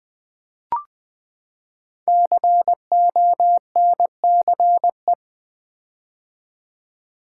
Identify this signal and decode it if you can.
Morse code, heard as a keyed tone, about 20 wpm, 700 Hz. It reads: CONCE
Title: Build A Word - 5 Letters At A Time - 10-Letter Words - 20wpm